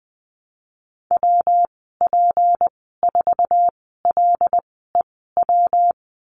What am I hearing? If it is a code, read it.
Morse code, heard as a keyed tone, about 19 wpm, spaced as for 14 wpm, 700 Hz. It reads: WP4LEW
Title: 173 Lesson 28g - Number 9 - Call Signs 20wpm